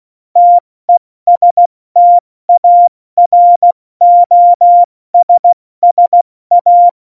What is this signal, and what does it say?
Morse code, heard as a keyed tone, about 16 wpm, 700 Hz. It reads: TESTAROSSA
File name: Morsekurs / Witze / Witz 04 August 2020, 70 F80